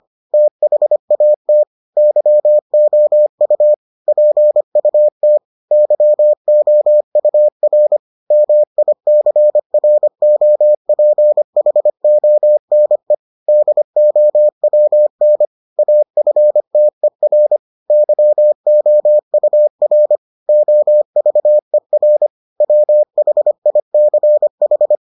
Morse code, heard as a keyed tone, about 25 wpm, 600 Hz. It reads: THAT YOU PUT YOUR MICROPHONE DOWN AFTER YOUR OVER WHICH